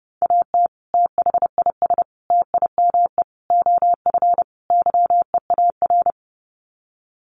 Morse code, heard as a keyed tone, about 28 wpm, 700 Hz. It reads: AT THIS TIME OF YEAR
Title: Sets of 5 Words Encore2 30wpm